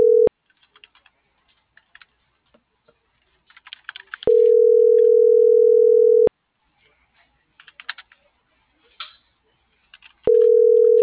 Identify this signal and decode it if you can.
Ringback tone (North America), 440 + 480 Hz.